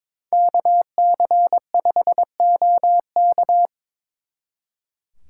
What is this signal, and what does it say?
Morse code, heard as a keyed tone, about 26 wpm, 700 Hz. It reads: KC5OK
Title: Speed Racing - USA 2x2 Call Signs 22wpm